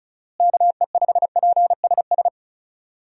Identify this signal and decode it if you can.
Morse code, heard as a keyed tone, about 34 wpm, 700 Hz. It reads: KE5PSS
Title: USA Call Signs - 6 Characters - Repeated 3x - 35wpm